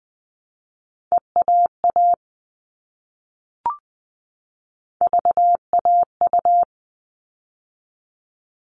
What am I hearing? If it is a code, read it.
Morse code, heard as a keyed tone, about 20 wpm, 700 Hz. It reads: EAA VAU
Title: All Cut Numbers - 3-Digit 20wpm